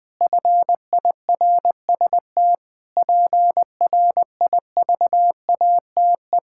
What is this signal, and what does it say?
Morse code, heard as a keyed tone, about 20 wpm, 700 Hz. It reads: FIRST PRIVATE
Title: Sentences from Top 1600 Words 20wpm